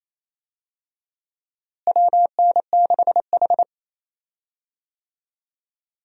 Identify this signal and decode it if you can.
Morse code, heard as a keyed tone, about 28 wpm, 700 Hz. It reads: WN6H
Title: USA Call Signs - 4 Characters - Repeated 3x - 28wpm